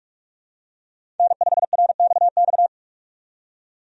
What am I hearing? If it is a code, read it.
Morse code, heard as a keyed tone, about 42 wpm, 700 Hz. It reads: N5RXX
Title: USA Call Signs - 5 Characters - 45wpm